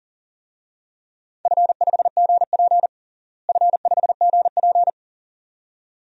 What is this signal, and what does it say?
Morse code, heard as a keyed tone, about 40 wpm, 700 Hz. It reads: F5GP F5GP
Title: DXCC-100 Call Signs - 4 Characters - Repeated 2x - 40wpm